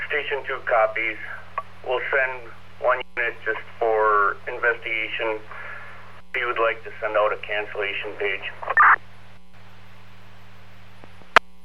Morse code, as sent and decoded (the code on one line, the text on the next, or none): none